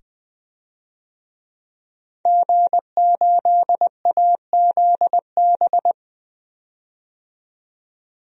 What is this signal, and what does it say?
Morse code, heard as a keyed tone, about 21 wpm, 700 Hz. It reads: G8AZB
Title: DXCC-100 Call Signs - 5 Characters - 20wpm